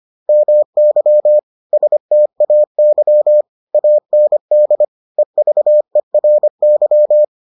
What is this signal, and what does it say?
Morse code, heard as a keyed tone, about 25 wpm, 600 Hz. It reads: MY STAY AND EVERY